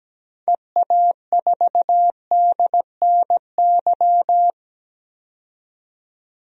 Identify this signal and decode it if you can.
Morse code, heard as a keyed tone, about 17 wpm, 700 Hz. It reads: EA4DNY
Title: DXCC-100 Call Signs - 6 Characters - 17wpm